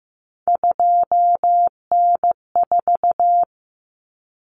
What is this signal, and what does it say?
Morse code, heard as a keyed tone, about 15 wpm, 700 Hz. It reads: 2N4
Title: Common Cut Numbers - 3-Digit 15wpm